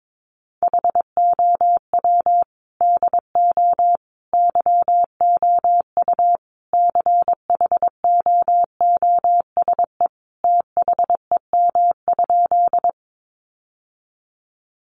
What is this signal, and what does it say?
Morse code, heard as a keyed tone, about 22 wpm, 700 Hz. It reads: HOW DO YOU CHOOSE THEM?
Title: Sentences from Top 700 Words 22wpm